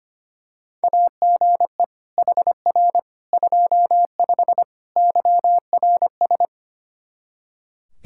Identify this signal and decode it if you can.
Morse code, heard as a keyed tone, about 25 wpm, 700 Hz. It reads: AGE HR 25 YRS
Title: K1USN SST QSO Practice 25wpm